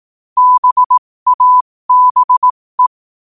Morse code, heard as a keyed tone, about 18 wpm, spaced as for 15 wpm, 1000 Hz. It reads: BABE
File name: qotdfiles / 15wpm